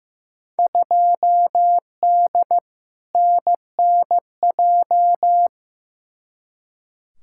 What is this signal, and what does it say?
Morse code, heard as a keyed tone, about 15 wpm, 700 Hz. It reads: 2D NNJ
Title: Field Day Exchanges - 15wpm